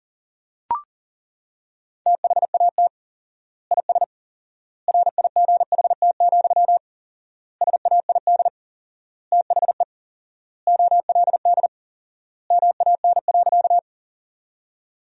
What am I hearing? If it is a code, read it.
Morse code, heard as a keyed tone, about 40 wpm, 700 Hz. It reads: THAT IS RIGHT, SAID THE OLD MAN.
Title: 4x Word-Spacing - Sentences from Top 200 Words - 40wpm